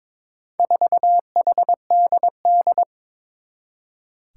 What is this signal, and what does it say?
Morse code, heard as a keyed tone, about 22 wpm, 700 Hz. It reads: 4HDD
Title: Four Challenging Letter-Number Combinations 22wpm